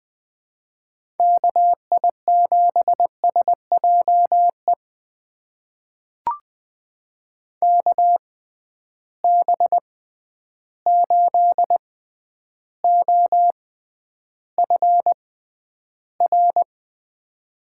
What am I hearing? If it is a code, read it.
Morse code, heard as a keyed tone, about 20 wpm, spaced as for 10 wpm, 700 Hz. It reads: KI7SJE KB8OFR
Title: USA Call Signs - 6x Character Spacing - 20wpm